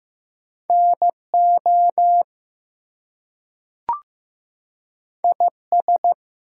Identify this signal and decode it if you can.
Morse code, heard as a keyed tone, about 15 wpm, 700 Hz. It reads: NO IS
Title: Two Letter Words - 15wpm